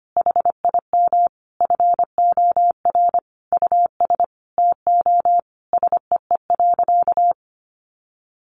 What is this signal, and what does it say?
Morse code, heard as a keyed tone, about 25 wpm, 700 Hz. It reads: HIM FOR US TO SEE.
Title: Sentences from Top 500 Words 25wpm